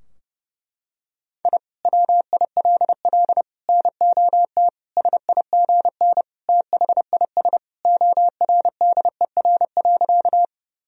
Morse code, heard as a keyed tone, about 30 wpm, 700 Hz. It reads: I WILL NOT SIGN THIS ORDER.
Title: Sentences from Top 1000 Words - Review Part 2 - 30wpm